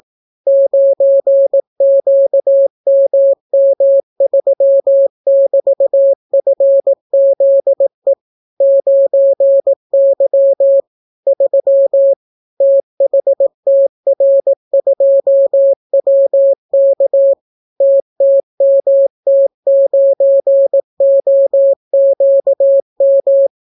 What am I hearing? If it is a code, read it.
Morse code, heard as a keyed tone, about 18 wpm, 550 Hz. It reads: 9QMM3<BT>FZE 9Y 3 THTR2WK TTMT9OQM